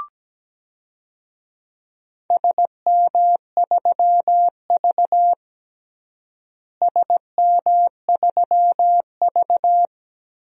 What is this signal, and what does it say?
Morse code, heard as a keyed tone, about 17 wpm, 700 Hz. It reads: SM3V SM3V
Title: DXCC-100 Call Signs - 4 Characters - Repeated 2x - 17wpm